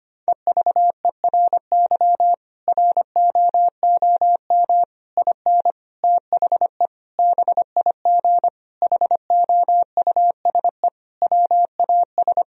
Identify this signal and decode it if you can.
Morse code, heard as a keyed tone, about 25 wpm, 700 Hz. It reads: EVERY ROOM IN THE BIG HOUSE WAS
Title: Sentences from Top 1000 Words - Review Part 3 - 25wpm